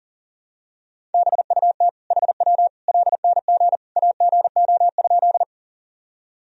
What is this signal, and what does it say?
Morse code, heard as a keyed tone, about 40 wpm, 700 Hz. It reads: BUT HW LNG AGO?
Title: Sentences from Top 500 Words and Abbreviations 40wpm